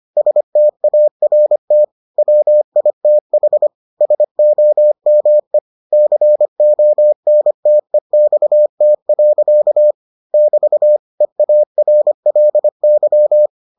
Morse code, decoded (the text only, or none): START WITH SOME CONTEXT. <BT> EARLY